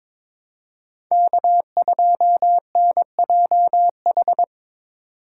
Morse code, heard as a keyed tone, about 22 wpm, 700 Hz. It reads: K2NJH
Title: USA Call Signs - 2x Character Spacing - 22wpm